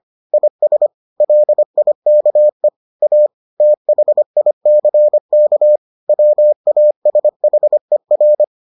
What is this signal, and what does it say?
Morse code, heard as a keyed tone, about 25 wpm, 600 Hz. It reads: IS LIKE A THICK WASHER